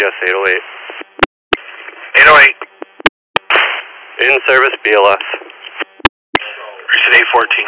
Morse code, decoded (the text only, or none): none